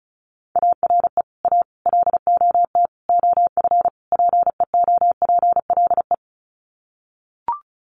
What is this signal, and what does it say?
Morse code, heard as a keyed tone, about 34 wpm, 700 Hz. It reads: ARE A LOT OF PEOPLE
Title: Sets of 5 Words 35wpm